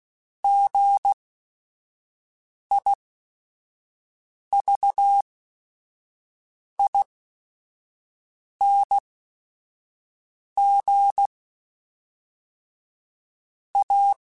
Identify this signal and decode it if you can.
Morse code, heard as a keyed tone, about 16 wpm, spaced as for 5 wpm, 800 Hz. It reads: GIVING A
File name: Technical Subjects / Technical Subjects-mp3s / XL @5